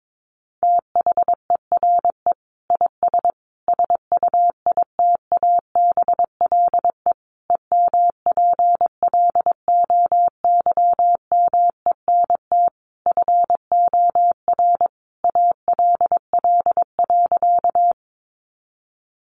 Morse code, heard as a keyed tone, about 22 wpm, 700 Hz. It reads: THERE IS SUITABLE EMPLOYMENT FOR ALL.
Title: Sentences from Top 1500 Words 22wpm